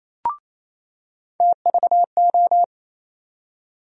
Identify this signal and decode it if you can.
Morse code, heard as a keyed tone, about 28 wpm, 700 Hz. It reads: TVO
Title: USA Call Sign Postfixes - 3 Letters 28wpm